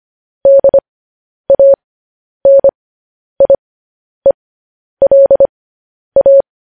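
Morse code, heard as a keyed tone, about 25 wpm, spaced as for 10 wpm, 550 Hz. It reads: DANIELA
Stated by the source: Morse code transmission of news headlines.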